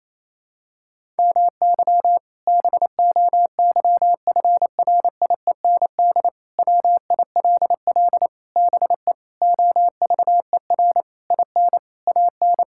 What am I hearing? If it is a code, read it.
Morse code, heard as a keyed tone, about 28 wpm, 700 Hz. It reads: MY BOYFRIEND WILL BE OVER IN AN
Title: Sentences from Top 1000 Words - Review Part 5 - 28wpm